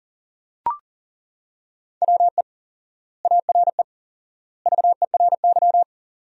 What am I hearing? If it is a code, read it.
Morse code, heard as a keyed tone, about 40 wpm, 700 Hz. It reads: WE ARE VERY
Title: 4x Word-Spacing - Sentences from Top 300 Words - 40wpm